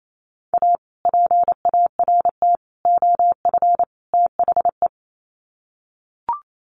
Morse code, heard as a keyed tone, about 28 wpm, 700 Hz. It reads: A PART OF THE
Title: Sets of 4 Words 28wpm